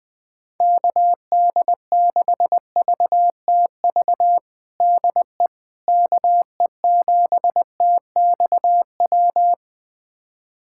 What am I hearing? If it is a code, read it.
Morse code, heard as a keyed tone, about 21 wpm, 700 Hz. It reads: KD6VTV DE KE7TXW
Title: USA Callsign DE Callsign 20wpm